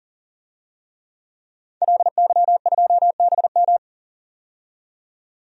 Morse code, heard as a keyed tone, about 37 wpm, 700 Hz. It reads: LY2BM
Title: DXCC-100 Call Signs - 5 Characters - 40wpm